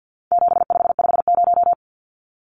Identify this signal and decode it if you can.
Morse code, heard as a keyed tone, about 50 wpm, 700 Hz. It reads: 7550
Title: Numbers - 4-Digit 50wpm